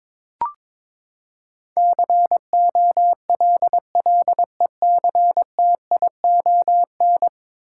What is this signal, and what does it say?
Morse code, heard as a keyed tone, about 22 wpm, 700 Hz. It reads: COLLECTION
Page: morse-code-ninja-practice-22wpm